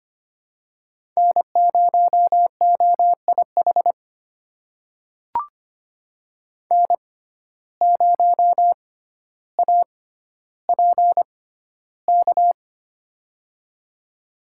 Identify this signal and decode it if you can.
Morse code, heard as a keyed tone, about 25 wpm, spaced as for 13 wpm, 700 Hz. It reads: N0OIH N0APK
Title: USA Call Signs - 5 Characters - 6x Character Spacing - 25wpm